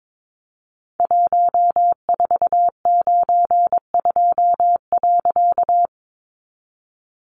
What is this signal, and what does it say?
Morse code, heard as a keyed tone, about 22 wpm, 700 Hz. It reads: 1492.
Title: A Sound Of Thunder 22wpm